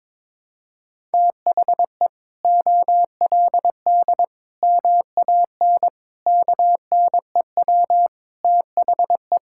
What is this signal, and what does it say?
Morse code, heard as a keyed tone, about 22 wpm, 700 Hz. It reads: THE OLD MAN KNEW THE
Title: Sentences from Top 1000 Words - Review Part 6 - 22wpm